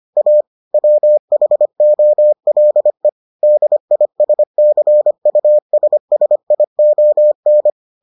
Morse code, heard as a keyed tone, about 25 wpm, 600 Hz. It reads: A WHOLE DISCUSSION